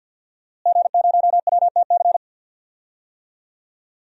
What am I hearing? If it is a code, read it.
Morse code, heard as a keyed tone, about 52 wpm, 700 Hz. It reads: G0WTC
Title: DXCC-100 Call Signs - 5 Characters - Repeated 2x - 50wpm